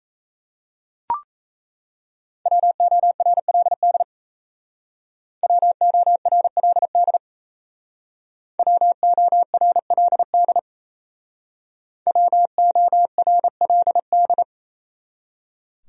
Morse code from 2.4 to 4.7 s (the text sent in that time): WORLD